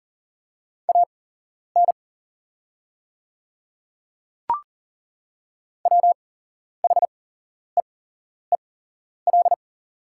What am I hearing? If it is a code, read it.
Morse code, heard as a keyed tone, about 38 wpm, spaced as for 11 wpm, 700 Hz. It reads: AN WHEEL
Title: 8x Character Spacing - Top 500 Words - 40wpm